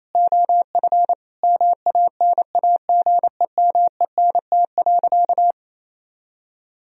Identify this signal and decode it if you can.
Morse code, heard as a keyed tone, about 28 wpm, 700 Hz. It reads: OF MANAGEMENT.
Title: Sentences from Top 1000 Words - Review Part 4 - 28wpm